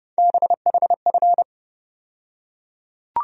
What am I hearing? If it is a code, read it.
Morse code, heard as a keyed tone, about 31 wpm, 700 Hz. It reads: BHF
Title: USA Call Sign Postfixes - 3 Letters 30wpm